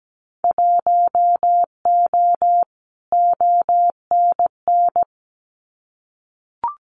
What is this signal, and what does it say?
Morse code, heard as a keyed tone, about 17 wpm, 700 Hz. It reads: 1O ONN
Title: Winter Field Day Exchanges Encore 17wpm